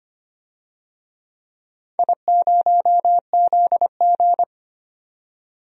Morse code, heard as a keyed tone, about 25 wpm, 700 Hz. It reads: I0ZG
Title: DXCC-100 Call Signs - 4 Characters - 25wpm